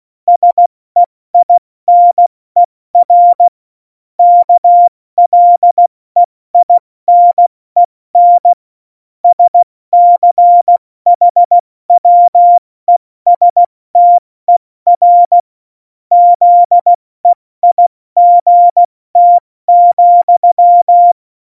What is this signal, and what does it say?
Morse code, heard as a keyed tone, about 16 wpm, 700 Hz. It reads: SEINER KLEINEN SCHWESTER ZEIGT,
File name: Morsekurs / Witze / Witz 14. November 2017, 70 F80